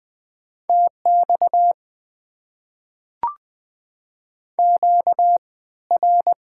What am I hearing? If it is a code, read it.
Morse code, heard as a keyed tone, about 21 wpm, spaced as for 15 wpm, 700 Hz. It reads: TX QR
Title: Top 90 QSO Elements - 3x Character Spacing - 20wpm